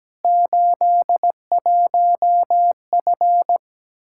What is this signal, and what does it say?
Morse code, heard as a keyed tone, about 17 wpm, 700 Hz. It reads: 81F
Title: Practice QSO - Exchange 2 - Info - 17wpm